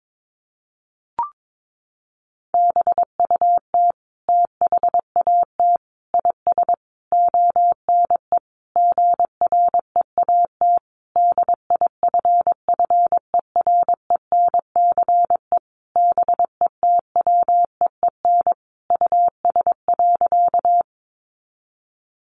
Morse code, heard as a keyed tone, about 22 wpm, 700 Hz. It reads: BUT THAT IS ONE GREAT DIFFERENCE BETWEEN US.
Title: Sentences from Top 800 Words 22wpm